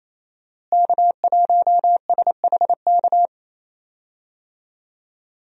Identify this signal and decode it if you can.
Morse code, heard as a keyed tone, about 28 wpm, 700 Hz. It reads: K1SHK
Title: USA Call Signs - 5 Characters - Repeated 2x - 28wpm